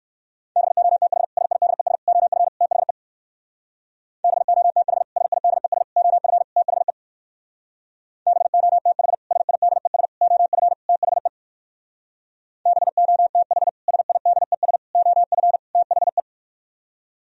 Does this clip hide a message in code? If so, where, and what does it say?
Yes: Morse code from 0.6 to 16.2 s, heard as a keyed tone, about 61 wpm, 700 Hz. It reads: BOTH SIDES OF THE BOTH SIDES OF THE BOTH SIDES OF THE BOTH SIDES OF THE